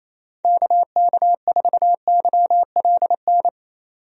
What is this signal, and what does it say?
Morse code, heard as a keyed tone, about 28 wpm, 700 Hz. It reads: KK4YLN
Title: USA Call Signs - 6 Characters - 28wpm